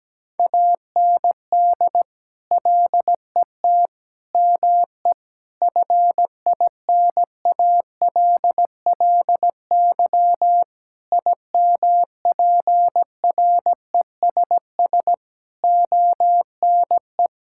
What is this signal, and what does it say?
Morse code, heard as a keyed tone, about 17 wpm, 700 Hz. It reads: AND LET ME FINALLY IMPRESS ONE
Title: Sentences from Top 1600 Words 17wpm